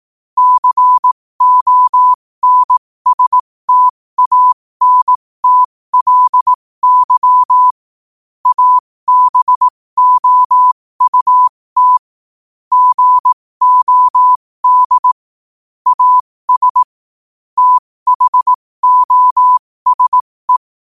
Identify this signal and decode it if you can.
Morse code, heard as a keyed tone, about 18 wpm, spaced as for 15 wpm, 1000 Hz. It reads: CONSTANTLY ABOUT GOD AS THOSE
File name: qotdfiles / 15wpm